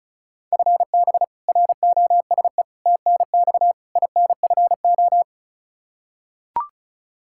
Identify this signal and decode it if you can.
Morse code, heard as a keyed tone, about 34 wpm, 700 Hz. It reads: FB ROSE TNX INFO
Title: Practice QSO - Exchange 3 - More Info - 35wpm